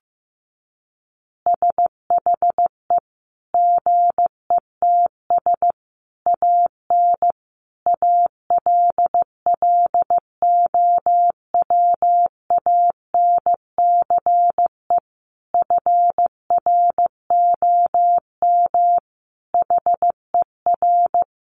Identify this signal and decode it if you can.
Morse code, heard as a keyed tone, about 15 wpm, 700 Hz. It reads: SHE GETS AN ALLOWANCE FROM HER
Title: Easy Phrases 15wpm